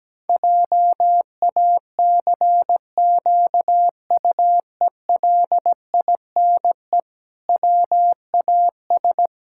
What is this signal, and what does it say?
Morse code, heard as a keyed tone, about 17 wpm, 700 Hz. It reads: JACQUELINE WAS
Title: Pangram Challenge v2 - 17wpm